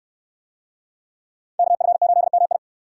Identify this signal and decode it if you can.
Morse code, heard as a keyed tone, about 63 wpm, 700 Hz. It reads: KF8MI